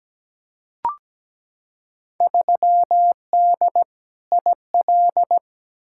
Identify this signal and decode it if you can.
Morse code, heard as a keyed tone, about 17 wpm, 700 Hz. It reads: 3D IL
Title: Field Day Exchanges Encore - 17wpm